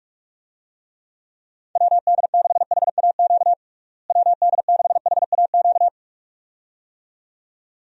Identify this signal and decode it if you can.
Morse code, heard as a keyed tone, about 46 wpm, 700 Hz. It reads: WD6HAQ WD6HAQ
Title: USA Call Signs - 6 Characters - Repeated 2x - 45wpm